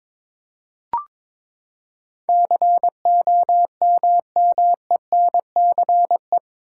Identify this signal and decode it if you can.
Morse code, heard as a keyed tone, about 22 wpm, 700 Hz. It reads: COMMENCE
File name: practice / 22wpm / Eight Letter Words - 22wpm